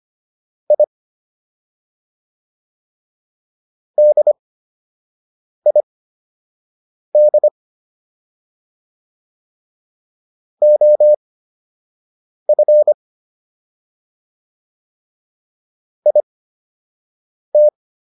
Morse code, heard as a keyed tone, about 25 wpm, spaced as for 6 wpm, 600 Hz. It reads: I DID OF IT